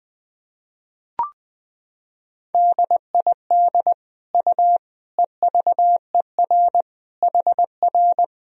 Morse code, heard as a keyed tone, about 20 wpm, 700 Hz. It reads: DID U EVER HR
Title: Sentences from Top 400 Words and Abbreviations 20wpm